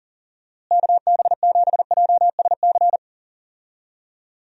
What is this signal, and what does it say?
Morse code, heard as a keyed tone, about 40 wpm, 700 Hz. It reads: KB7JSC
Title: USA Call Signs - 6 Characters - Repeated 2x - 40wpm